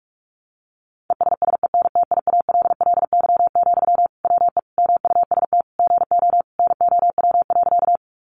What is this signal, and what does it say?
Morse code, heard as a keyed tone, about 46 wpm, 700 Hz. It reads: ESSENTIALLY, WE MUST GO NOW.
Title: Sentences from Top 1300 Words 45wpm